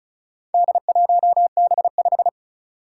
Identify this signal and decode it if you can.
Morse code, heard as a keyed tone, about 36 wpm, 700 Hz. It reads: D1B5
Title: Four Challenging Letter-Number Combinations 35wpm